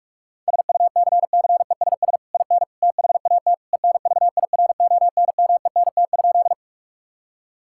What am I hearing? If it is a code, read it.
Morse code, heard as a keyed tone, about 46 wpm, 700 Hz. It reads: SUCCESS IN THAT ENVIRONMENT?
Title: Sentences from Top 1000 Words - Review Part 5 - 45wpm